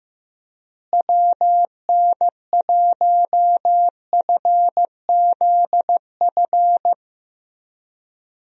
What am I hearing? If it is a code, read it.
Morse code, heard as a keyed tone, about 15 wpm, 700 Hz. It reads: WN1FZF
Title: USA Call Signs - 6 Characters - Repeated 2x - 15wpm